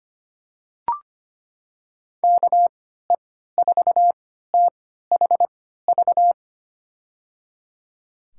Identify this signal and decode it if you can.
Morse code, heard as a keyed tone, about 25 wpm, spaced as for 14 wpm, 700 Hz. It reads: KE4THV
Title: USA Call Signs - 6 Characters - 3x Character Spacing - 25wpm